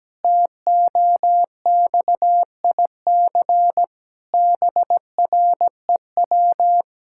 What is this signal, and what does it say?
Morse code, heard as a keyed tone, about 17 wpm, 700 Hz. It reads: TOXIC BREW
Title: Pangram Challenge v2 - 17wpm